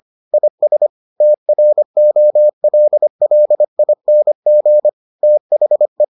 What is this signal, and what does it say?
Morse code, heard as a keyed tone, about 25 wpm, 600 Hz. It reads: IS TROLLING THE